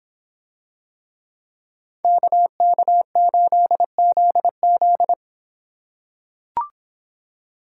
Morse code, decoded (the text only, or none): KK8ZZ